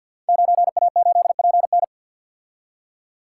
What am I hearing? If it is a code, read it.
Morse code, heard as a keyed tone, about 52 wpm, 700 Hz. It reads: 9A8PN
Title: Code Groups - 5 Letter:Numbers - 50wpm